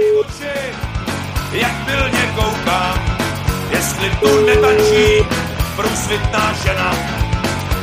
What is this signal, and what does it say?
Ringback tone (Europe, 425 Hz), with speech.